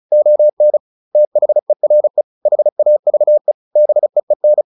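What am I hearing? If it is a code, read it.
Morse code, heard as a keyed tone, about 34 wpm, 600 Hz. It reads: ON THERE HAVE BEEN
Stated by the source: Morse practice at 35 wpm from CW med Gnister, sending Ken Follett's novel Never in English.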